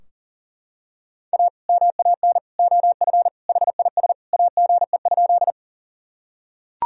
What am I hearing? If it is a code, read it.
Morse code, heard as a keyed tone, about 40 wpm, 700 Hz. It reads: A MAN OF HIS AGE?
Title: Sentences from Top 1000 Words - Review Part 3 - 40wpm